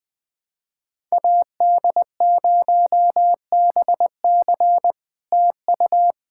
Morse code, heard as a keyed tone, about 20 wpm, 700 Hz. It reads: AD0BC TU
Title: POTA QSO Practice - 20wpm